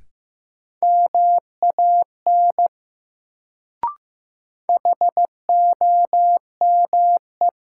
Morse code, heard as a keyed tone, about 15 wpm, 700 Hz. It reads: MAN HOME
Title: Sentences from Top 200 Words - New Word Drill - 15wpm